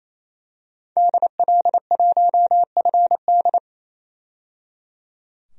Morse code, heard as a keyed tone, about 27 wpm, 700 Hz. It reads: DL1FD
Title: DXCC-100 Call Signs - 5 Characters - 28wpm